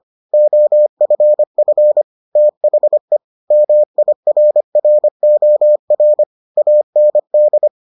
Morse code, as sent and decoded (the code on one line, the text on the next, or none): --- ..-. ..-. / - .... / -- .. .-. .-. --- .-. / .- -. -..
OFF THE MIRROR AND